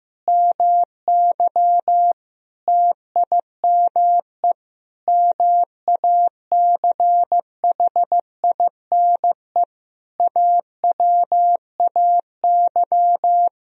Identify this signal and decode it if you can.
Morse code, heard as a keyed tone, about 16 wpm, 700 Hz. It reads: MY TIME MACHINE AWAY
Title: Sentences from Top 1000 Words - Review Part 2 - 15wpm